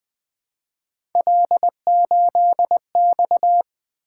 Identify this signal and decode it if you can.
Morse code, heard as a keyed tone, about 20 wpm, 700 Hz. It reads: L8X
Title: Three Letter-Number Combinations - 20wpm